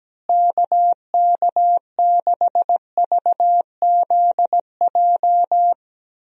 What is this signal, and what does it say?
Morse code, heard as a keyed tone, about 17 wpm, 700 Hz. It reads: KK6VZJ
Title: USA Call Signs - 6 Characters - Repeated 2x - 17wpm